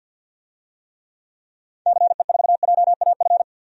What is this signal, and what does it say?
Morse code, heard as a keyed tone, about 46 wpm, 700 Hz. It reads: CE4JAF
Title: DXCC-100 Call Signs - 6 Characters - 50wpm